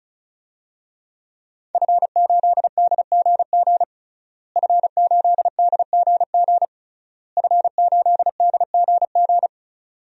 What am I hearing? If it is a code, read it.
Morse code, heard as a keyed tone, about 36 wpm, 700 Hz. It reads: F8DGG F8DGG F8DGG
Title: DXCC-100 Call Signs - 5 Characters - Repeated 3x - 35wpm